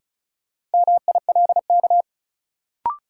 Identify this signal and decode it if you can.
Morse code, heard as a keyed tone, about 36 wpm, 700 Hz. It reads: MILK
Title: Sentences from Top 1500 Words - New Word Drill - 35wpm